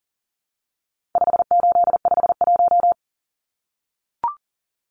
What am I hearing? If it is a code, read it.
Morse code, heard as a keyed tone, about 40 wpm, 700 Hz. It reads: 5851
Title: Numbers - 4-Digit 40wpm